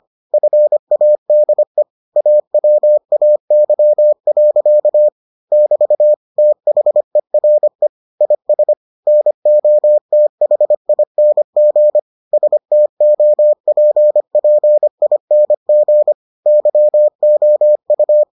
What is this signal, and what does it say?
Morse code, heard as a keyed tone, about 25 wpm, 600 Hz. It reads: FADE AWAY. <BT> THERE IS NOTHING STOPPING YOU